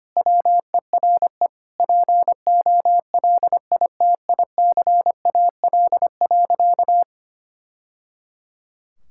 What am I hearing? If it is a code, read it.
Morse code, heard as a keyed tone, about 25 wpm, 700 Hz. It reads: WERE POLITICAL.
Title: Sentences from Top 1000 Words - Review Part 5 - 25wpm